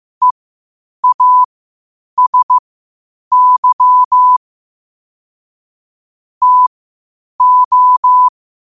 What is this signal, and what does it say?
Morse code, heard as a keyed tone, about 15 wpm, spaced as for 8 wpm, 1000 Hz. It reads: EASY TO